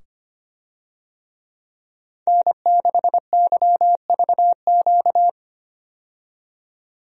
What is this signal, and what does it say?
Morse code, heard as a keyed tone, about 25 wpm, 700 Hz. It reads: N6YVQ
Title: USA Call Signs - 5 Characters - 25wpm